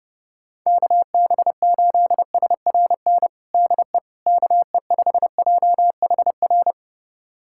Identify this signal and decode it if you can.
Morse code, heard as a keyed tone, about 30 wpm, 700 Hz. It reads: KB8SRN DE KE5JHR
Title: USA Callsign DE Callsign - 6 Characters - 30wpm